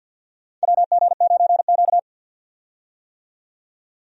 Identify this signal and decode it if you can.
Morse code, heard as a keyed tone, about 52 wpm, 700 Hz. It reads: WG9Q
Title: USA Call Signs - 4 Characters - 50wpm